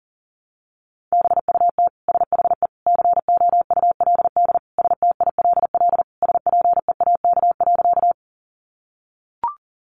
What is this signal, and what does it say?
Morse code, heard as a keyed tone, about 37 wpm, 700 Hz. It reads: BUT SHE COULD STILL SPEAK.
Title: Sentences from Top 1100 Words 40wpm